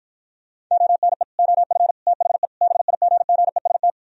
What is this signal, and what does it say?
Morse code, heard as a keyed tone, about 53 wpm, 700 Hz. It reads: ONE OF THE BIGGEST